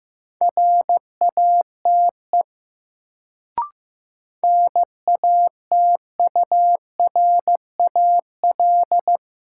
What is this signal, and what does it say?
Morse code, heard as a keyed tone, about 15 wpm, 700 Hz. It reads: RATE NATURAL
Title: Sentences from Top 700 Words - New Word Drill - 15wpm